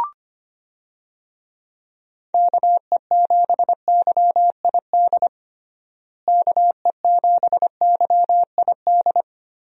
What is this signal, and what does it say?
Morse code, heard as a keyed tone, about 25 wpm, 700 Hz. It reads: KE7YID KE7YID
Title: USA Call Signs - 6 Characters - Repeated 3x - 25wpm